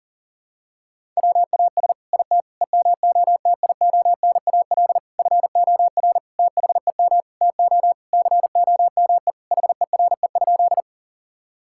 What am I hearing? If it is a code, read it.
Morse code, heard as a keyed tone, about 37 wpm, 690 Hz. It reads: WAS IT EMOTIONAL FOR THEM TO COME HERE?
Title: Sentences from Top 1000 Words - Review Part 5 - 40wpm